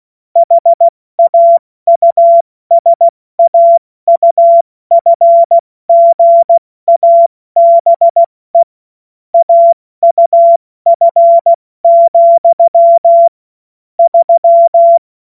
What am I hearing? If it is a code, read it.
Morse code, heard as a keyed tone, about 16 wpm, 680 Hz. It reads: HAUSAUFGABE AUF, 3
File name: Morsekurs / Witze / Witz 06. Juli 2021, 70 F80